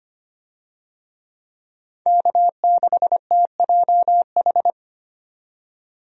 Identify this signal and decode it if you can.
Morse code, heard as a keyed tone, about 25 wpm, 700 Hz. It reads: K6TJH
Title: USA Call Signs - 5 Characters - 25wpm